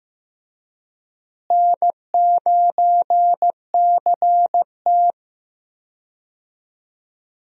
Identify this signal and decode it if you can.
Morse code, heard as a keyed tone, about 14 wpm, 700 Hz. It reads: N9CT
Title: USA Call Signs - 4 Characters - Repeated 3x - 15wpm